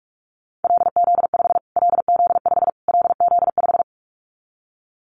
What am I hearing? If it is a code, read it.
Morse code, heard as a keyed tone, about 44 wpm, 700 Hz. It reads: LZ5 LZ5 LZ5
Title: Callsign Prefixes for Top 25 DXCC - Repeated 3x - 45wpm